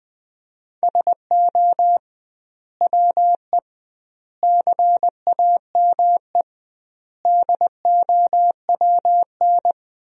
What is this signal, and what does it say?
Morse code, heard as a keyed tone, about 20 wpm, 700 Hz. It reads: SO WE CAME DOWN
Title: 2x Word-Spacing - Sentences from Top 400 Words - 20wpm